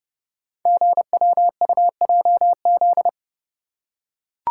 Morse code, heard as a keyed tone, about 30 wpm, 700 Hz. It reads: GWUJZ
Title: Code Groups - 5 Letters - 30wpm